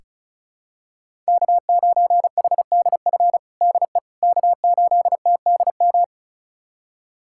Morse code, heard as a keyed tone, about 36 wpm, 700 Hz. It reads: K9HDF DE K8TDM
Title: USA Callsign DE Callsign - 5 Characters - 35wpm